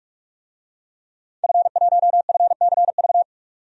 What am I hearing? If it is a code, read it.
Morse code, heard as a keyed tone, about 46 wpm, 700 Hz. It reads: F1FCV
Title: DXCC-100 Call Signs - 5 Characters - 45wpm